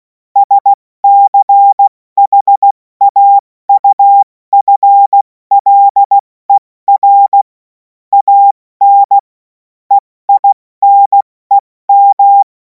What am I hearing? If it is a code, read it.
Morse code, heard as a keyed tone, about 16 wpm, 800 Hz. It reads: SCHAUFLER AN EINEM
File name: Morsekurs / Witze / Witz 17. Februar 2026 70 F80